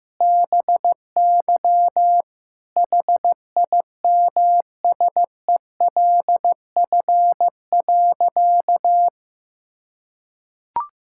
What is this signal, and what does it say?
Morse code, heard as a keyed tone, about 16 wpm, 700 Hz. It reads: BY HIMSELF.
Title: Sentences from Top 1600 Words 15wpm